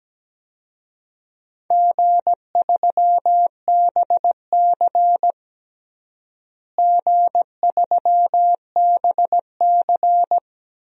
Morse code, heard as a keyed tone, about 17 wpm, 700 Hz. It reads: G3BC G3BC
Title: DXCC-100 Call Signs - 4 Characters - Repeated 3x - 17wpm